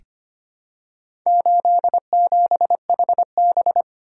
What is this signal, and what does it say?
Morse code, heard as a keyed tone, about 25 wpm, 700 Hz. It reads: 87HB
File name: practice / 25wpm / Four Letter-Number Combinations - 25wpm